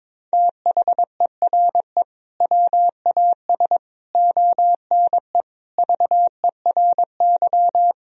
Morse code, heard as a keyed tone, about 22 wpm, 700 Hz. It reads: THERE WAS ONE VERY
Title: Sentences from Top 1000 Words - Review Part 3 - 22wpm